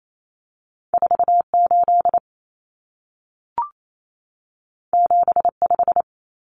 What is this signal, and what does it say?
Morse code, heard as a keyed tone, about 28 wpm, 700 Hz. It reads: 48 75
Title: Common Cut Numbers - 2-Digit 28wpm